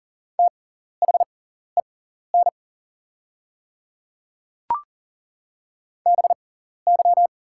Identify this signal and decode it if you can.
Morse code, heard as a keyed tone, about 41 wpm, spaced as for 14 wpm, 700 Hz. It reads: THEN BY
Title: 6x Character Spacing - Top 100 Words - 40wpm